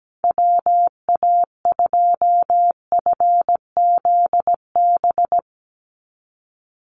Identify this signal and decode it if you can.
Morse code, heard as a keyed tone, about 17 wpm, 700 Hz. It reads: WA2FZB
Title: Call Signs - 17wpm